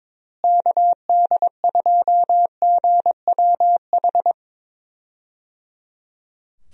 Morse code, heard as a keyed tone, about 21 wpm, 700 Hz. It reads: KD2GWH
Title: USA Call Signs - 6 Characters - Repeated 3x - 22wpm